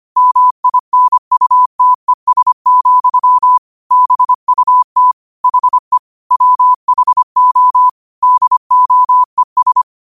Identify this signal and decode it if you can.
Morse code, heard as a keyed tone, about 25 wpm, 1000 Hz. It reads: MINUTES, BUT HE WHO DOES